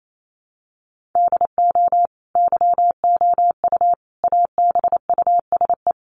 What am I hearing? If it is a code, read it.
Morse code, heard as a keyed tone, about 28 wpm, 700 Hz. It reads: DO YOU ABUSE
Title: Sentences from Top 1300 Words 28wpm